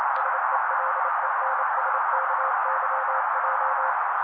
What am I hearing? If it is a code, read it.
Morse code, heard as a keyed tone, about 28 wpm, 530 Hz. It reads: SELENSKYJ
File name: daily / 27wpm noise